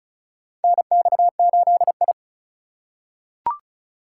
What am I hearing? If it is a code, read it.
Morse code, heard as a keyed tone, about 36 wpm, 700 Hz. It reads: NX8I
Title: USA Call Signs - 4 Characters - 2x Character Spacing - 35wpm